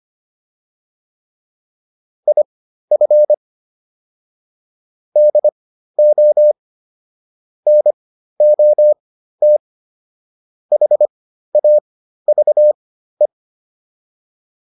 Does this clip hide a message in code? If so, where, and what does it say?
Yes: Morse code from 2.3 to 13.3 s, heard as a keyed tone, about 25 wpm, spaced as for 13 wpm, 600 Hz. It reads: IF DO NOT HAVE